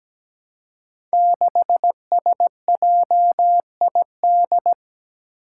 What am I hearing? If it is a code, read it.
Morse code, heard as a keyed tone, about 17 wpm, 700 Hz. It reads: 6SJID
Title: Five Challenging Letter-Number Combinations 17wpm